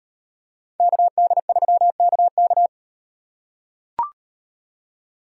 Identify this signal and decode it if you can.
Morse code, heard as a keyed tone, about 38 wpm, 700 Hz. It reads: KD3KK